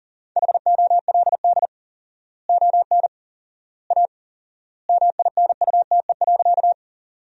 Morse code, heard as a keyed tone, about 40 wpm, 700 Hz. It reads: HOLD ON A MINUTE.
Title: 4x Word-Spacing - Sentences from Top 500 Words - 40wpm